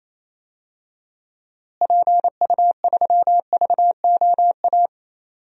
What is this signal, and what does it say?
Morse code, heard as a keyed tone, about 28 wpm, 700 Hz. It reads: PU3VOA
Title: DXCC-100 Call Signs - 6 Characters - Repeated 3x - 28wpm